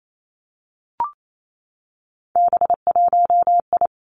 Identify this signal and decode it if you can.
Morse code, heard as a keyed tone, about 28 wpm, 700 Hz. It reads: B1I
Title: Three Challenging Letter-Number Combinations 28wpm